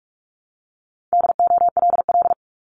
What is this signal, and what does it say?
Morse code, heard as a keyed tone, about 46 wpm, 700 Hz. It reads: DOLL
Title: Multi-letter Phonemes 45wpm